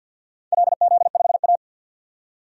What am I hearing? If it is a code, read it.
Morse code, heard as a keyed tone, about 52 wpm, 700 Hz. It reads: LZ5A
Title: DXCC-100 Call Signs - 4 Characters - 50wpm